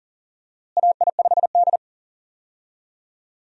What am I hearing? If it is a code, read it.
Morse code, heard as a keyed tone, about 40 wpm, 700 Hz. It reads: AI5D
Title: USA Call Signs - 4 Characters - Repeated 3x - 40wpm